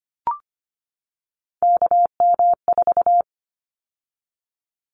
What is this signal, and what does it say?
Morse code, heard as a keyed tone, about 25 wpm, 700 Hz. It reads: KM4
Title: Callsign Prefixes for Top 25 DXCC - 25wpm